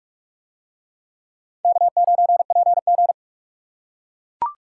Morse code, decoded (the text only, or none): K9PG